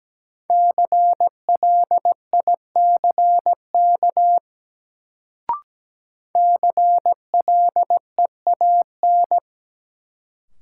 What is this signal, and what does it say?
Morse code, heard as a keyed tone, about 17 wpm, 700 Hz. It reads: CLICK CLEAN
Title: Sentences from Top 1100 Words - New Word Drill - 17wpm